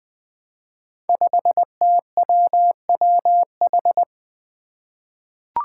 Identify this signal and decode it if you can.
Morse code, heard as a keyed tone, about 20 wpm, 700 Hz. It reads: 5TWWH
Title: Five Challenging Letter-Number Combinations 20wpm